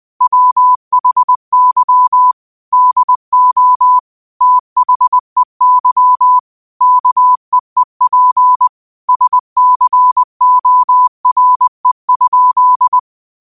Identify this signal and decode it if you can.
Morse code, heard as a keyed tone, about 20 wpm, 1000 Hz. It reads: WHY DO THEY KEEP SCORE?
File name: qotdfiles / 20wpm